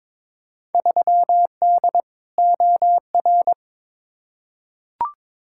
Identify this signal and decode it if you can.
Morse code, heard as a keyed tone, about 22 wpm, 700 Hz. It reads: 3D OR
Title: Field Day Exchanges - 22wpm